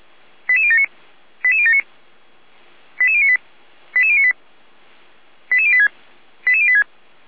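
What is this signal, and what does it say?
Five-tone selective call (ZVEI), heard as quick runs of five tones, tones of about 80 ms. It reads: 88079 88079 88098 88098 88086 88086
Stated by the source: ZVEI five-tone call received on an SDR.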